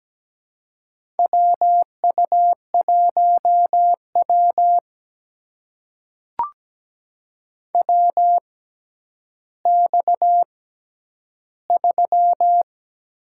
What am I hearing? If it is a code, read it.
Morse code, heard as a keyed tone, about 17 wpm, spaced as for 10 wpm, 700 Hz. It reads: WU1W WX3
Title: USA Call Signs - 4 Characters - 6x Character Spacing - 17wpm